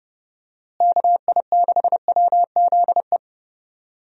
Morse code, heard as a keyed tone, about 30 wpm, 700 Hz. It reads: KI6WZE
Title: USA Call Signs - 6 Characters - Repeated 2x - 30wpm